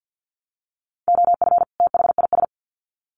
Morse code, heard as a keyed tone, about 50 wpm, 700 Hz. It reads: OF THIS
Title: Sets of 2 Words 50wpm